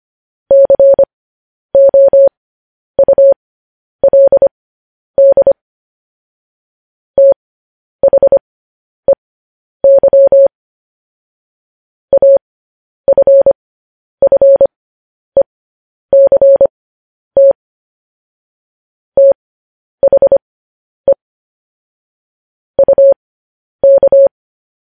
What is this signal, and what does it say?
Morse code, heard as a keyed tone, about 25 wpm, spaced as for 10 wpm, 550 Hz. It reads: COULD THEY AFFECT THE UK
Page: News Headlines in Morse Code at 10 WPM